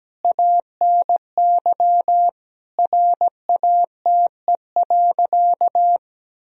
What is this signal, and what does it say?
Morse code, heard as a keyed tone, about 17 wpm, 700 Hz. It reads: ANY RATE.
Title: Sentences from Top 1000 Words - Review Part 4 - 17wpm